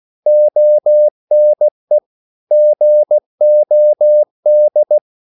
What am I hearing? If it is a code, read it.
Morse code, heard as a keyed tone, about 16 wpm, 600 Hz. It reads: ONE GOD
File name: CW / Practice / EnglishQuotations0116